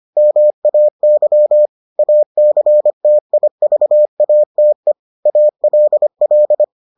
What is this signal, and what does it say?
Morse code, heard as a keyed tone, about 25 wpm, 600 Hz. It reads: MAY ACTIVATE ALL